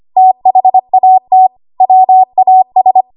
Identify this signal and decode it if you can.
Morse code, heard as a keyed tone, about 25 wpm, 750 Hz. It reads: THAT WAS
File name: Archive / 25 WPM / 25WPM